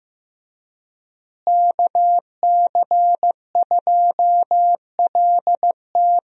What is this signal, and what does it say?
Morse code, heard as a keyed tone, about 15 wpm, 700 Hz. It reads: KC2LT
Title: USA Call Signs - 5 Characters - Repeated 2x - 15wpm